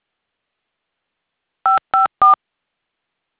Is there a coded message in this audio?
Yes: DTMF from 1.6 to 2.3 s, heard as telephone touch tones, tones of about 140 ms. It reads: 554